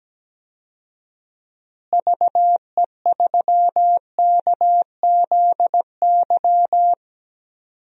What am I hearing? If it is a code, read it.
Morse code, heard as a keyed tone, about 17 wpm, 700 Hz. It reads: VE3KZY
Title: DXCC-100 Call Signs - 6 Characters - Repeated 2x - 17wpm